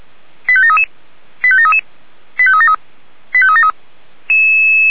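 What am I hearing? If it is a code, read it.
Five-tone selective call (ZVEI), heard as quick runs of five tones, tones of about 70 ms. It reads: 85733 85733 85373 85373
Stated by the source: ZVEI five-tone call received by SDR.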